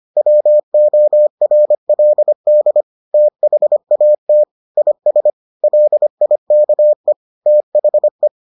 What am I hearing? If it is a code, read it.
Morse code, heard as a keyed tone, about 25 wpm, 600 Hz. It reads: WORLD THAT IS LIKE THE